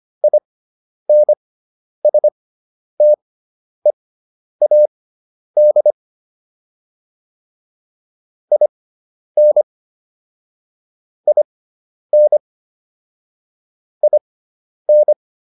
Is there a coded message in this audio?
Yes: Morse code from 0.2 to 15.1 s, heard as a keyed tone, about 25 wpm, spaced as for 10 wpm, 600 Hz. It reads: INSTEAD IN IN IN